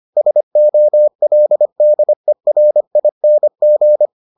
Morse code, heard as a keyed tone, about 25 wpm, 600 Hz. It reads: SOLDERING